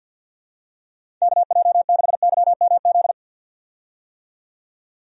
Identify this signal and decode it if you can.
Morse code, heard as a keyed tone, about 47 wpm, 700 Hz. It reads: KJ6YMZ
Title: USA Call Signs - 6 Characters - 50wpm